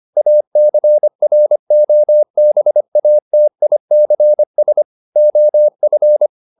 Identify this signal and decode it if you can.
Morse code, heard as a keyed tone, about 25 wpm, 600 Hz. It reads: ACROBATICS OF